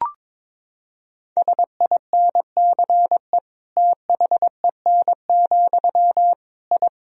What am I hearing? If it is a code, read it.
Morse code, heard as a keyed tone, about 22 wpm, 700 Hz. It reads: SINCE THEN, I
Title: Sentences from Top 1000 Words - Review Part 3 - 22wpm